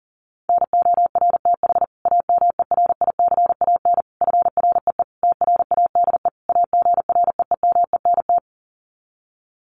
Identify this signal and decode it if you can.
Morse code, heard as a keyed tone, about 40 wpm, 700 Hz. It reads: NORTH AMERICAN FREE TRADE AGREEMENT
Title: Sets of 5 Words Encore 40wpm